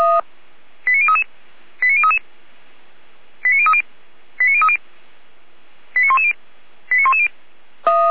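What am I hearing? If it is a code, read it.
Five-tone selective call (ZVEI), heard as quick runs of five tones, tones of about 70 ms. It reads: 89033 89033 89030 89030 89110 89110
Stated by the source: ZVEI five-tone call 89140 received by SDR, from the feuerwehr-orlamuende.org fire brigade server.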